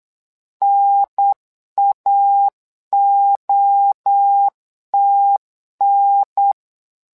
Morse code, heard as a keyed tone, about 8 wpm, 800 Hz. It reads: NAOTN
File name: LEZIONI / 3 MWRO / 3 MWRO 8wpm 2606